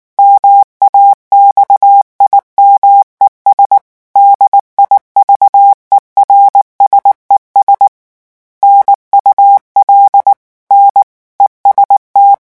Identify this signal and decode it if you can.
Morse code, heard as a keyed tone, about 19 wpm, 800 Hz. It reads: MAXIMES DIVERSES NUL N EST
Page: REVISION 8 - 8 mots minute